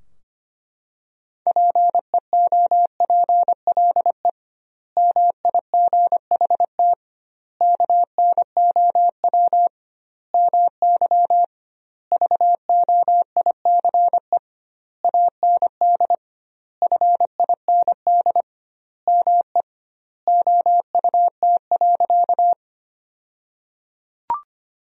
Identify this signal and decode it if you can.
Morse code, heard as a keyed tone, about 25 wpm, 700 Hz. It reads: PEOPLE MIGHT KNOW MY VOICE AND FIND ME OUT.
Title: 2x Word-Spacing - Sentences from Top 500 Words - 25wpm